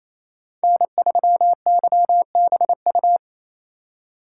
Morse code, heard as a keyed tone, about 28 wpm, 700 Hz. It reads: N3YBU
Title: USA Call Signs - 5 Characters - 8x Character Spacing - 28wpm